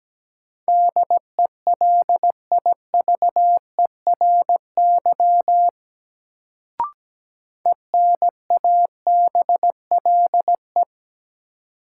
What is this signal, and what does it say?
Morse code, heard as a keyed tone, about 17 wpm, 700 Hz. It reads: DELIVERY ENABLE
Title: Sentences from Top 1500 Words - New Word Drill - 17wpm